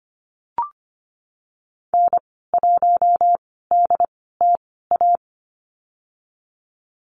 Morse code, heard as a keyed tone, about 25 wpm, spaced as for 16 wpm, 700 Hz. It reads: N1DTA
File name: Lesson 15g - Number 5 - Call Signs 25wpm